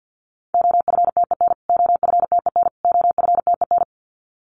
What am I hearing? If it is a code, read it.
Morse code, heard as a keyed tone, about 50 wpm, 700 Hz. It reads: OFTEN OFTEN OFTEN
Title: Top 500 Words Repeated 50wpm